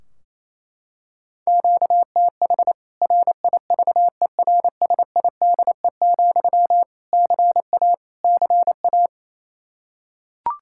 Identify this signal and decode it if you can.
Morse code, heard as a keyed tone, about 28 wpm, 700 Hz. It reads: QTH RIVERSIDE, CA CA
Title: K1USN SST QSO Practice 28wpm